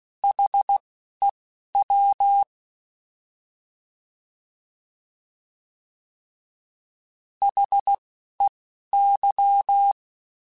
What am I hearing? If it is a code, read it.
Morse code, heard as a keyed tone, about 16 wpm, spaced as for 11 wpm, 800 Hz. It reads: HEW HEY
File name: Alphabetical / Alphabetical mp3s / Hs @10